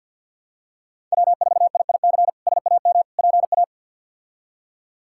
Morse code, heard as a keyed tone, about 51 wpm, 700 Hz. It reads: W4IIC SAM PA
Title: K1USN SST QSO Practice - 50wpm